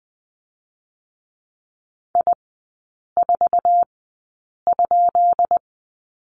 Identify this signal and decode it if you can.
Morse code, heard as a keyed tone, about 20 wpm, spaced as for 8 wpm, 700 Hz. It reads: I4?
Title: 133 Lesson 22e - Question Mark - Three Characters 20wpm